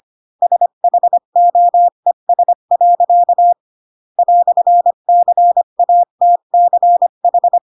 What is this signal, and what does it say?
Morse code, heard as a keyed tone, about 25 wpm, 700 Hz. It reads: SHOES. "CATCH
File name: Pinocchio / Pinocchio ch3 20 25 ik1hge